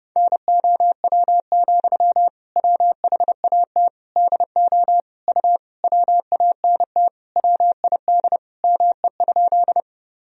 Morse code, heard as a keyed tone, about 30 wpm, 700 Hz. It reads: NOW, WHAT DO U WANT WID ME?
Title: Sentences from Top 200 Words and Abbreviations 30wpm